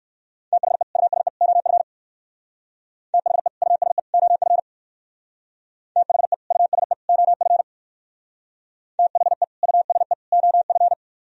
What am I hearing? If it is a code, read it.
Morse code, heard as a keyed tone, about 58 wpm, 700 Hz. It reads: THE USE OF THE USE OF THE USE OF THE USE OF